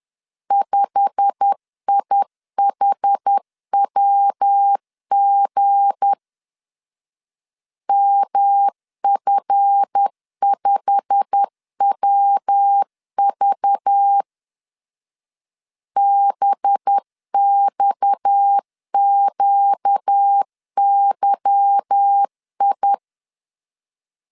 Morse code, heard as a keyed tone, about 10 wpm, 800 Hz. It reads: 5IHWG MF5WV BXQYI
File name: LEZIONI / 8 YJ45 / 8 YJ45 10wpm 2549